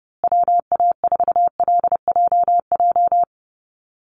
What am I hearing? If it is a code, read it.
Morse code, heard as a keyed tone, about 30 wpm, 700 Hz. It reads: WA4LJJ
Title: Call Signs Repeated 3x - 30wpm